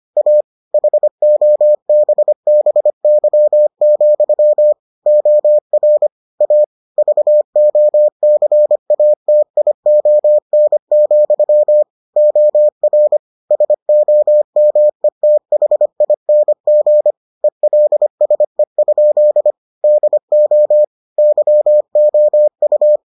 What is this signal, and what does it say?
Morse code, heard as a keyed tone, about 25 wpm, 600 Hz. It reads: A HOBBY, OR A VOCATION, OR SOMETHING ELSE? DO YOU